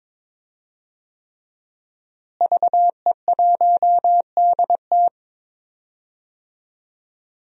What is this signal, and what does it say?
Morse code, heard as a keyed tone, about 22 wpm, 700 Hz. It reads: VE1DT
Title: DXCC-100 Call Signs - 5 Characters - Repeated 3x - 22wpm